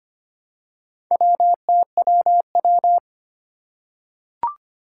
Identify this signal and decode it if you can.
Morse code, heard as a keyed tone, about 25 wpm, 700 Hz. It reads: WTWW
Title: Four Challenging Letter-Number Combinations 25wpm